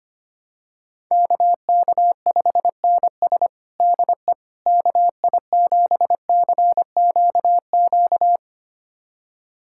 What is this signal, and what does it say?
Morse code, heard as a keyed tone, about 25 wpm, 700 Hz. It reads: KK5NS DE KI7CQQ
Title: USA Callsign DE Callsign 25wpm